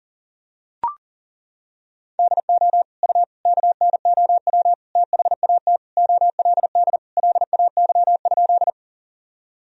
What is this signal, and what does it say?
Morse code, heard as a keyed tone, about 40 wpm, 700 Hz. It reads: DO U KNOW THAT OLD LAY?
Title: Sentences from Top 400 Words and Abbreviations 40wpm